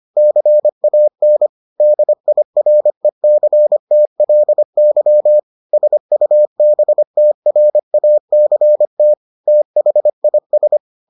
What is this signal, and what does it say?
Morse code, heard as a keyed tone, about 25 wpm, 600 Hz. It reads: CAN DIRECTLY SUBTRACT THIS